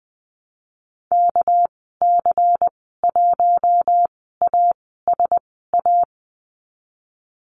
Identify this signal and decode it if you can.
Morse code, heard as a keyed tone, about 20 wpm, spaced as for 15 wpm, 700 Hz. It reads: KC1ASA
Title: 207 Lesson 33g - Letter j - Call Signs 20wpm